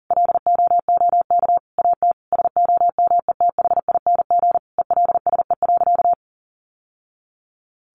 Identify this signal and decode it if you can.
Morse code, heard as a keyed tone, about 40 wpm, 700 Hz. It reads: LOOK AT SOMETHING ELSE.
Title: Sentences from Top 800 Words 40wpm